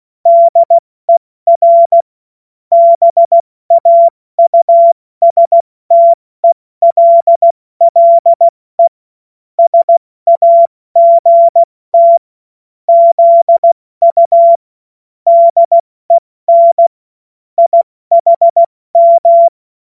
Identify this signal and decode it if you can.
Morse code, heard as a keyed tone, about 16 wpm, 680 Hz. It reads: DER BAUSTELLE SAGT ZU DEN IHM
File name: Morsekurs / Witze / Witz 01. August 2023 70 F80